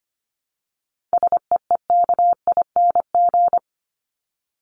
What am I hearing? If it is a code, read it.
Morse code, heard as a keyed tone, about 25 wpm, 700 Hz. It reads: SEEKING
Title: ING Suffix - Top 100 Words with Base Words 25wpm